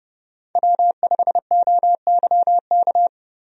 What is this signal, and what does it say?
Morse code, heard as a keyed tone, about 30 wpm, 700 Hz. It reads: W5OYK
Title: USA Call Signs - 5 Characters - 30wpm